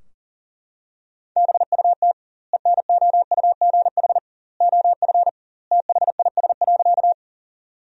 Morse code, heard as a keyed tone, about 40 wpm, 700 Hz. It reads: BUT ENOUGH OF THIS.
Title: 2x Word-Spacing - Sentences from Top 400 Words - 40wpm